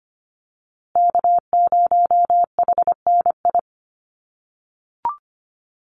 Morse code, heard as a keyed tone, about 25 wpm, 700 Hz. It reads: K0HNI
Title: USA Call Signs Repeated 3x 25wpm